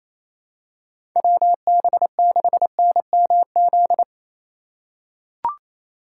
Morse code, heard as a keyed tone, about 28 wpm, 700 Hz. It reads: WB6NMZ
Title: USA Call Signs - 6 Characters - 4x Character Spacing - 28wpm